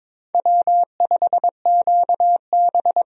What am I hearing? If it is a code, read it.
Morse code, heard as a keyed tone, about 21 wpm, 700 Hz. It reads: W5QB
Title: USA Call Signs - 4 Characters - Repeated 3x - 22wpm